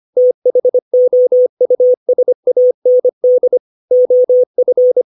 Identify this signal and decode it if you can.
Morse code, heard as a keyed tone, about 25 wpm, 500 Hz. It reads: THOUSAND OF
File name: 632 words / 632 Words 25WPM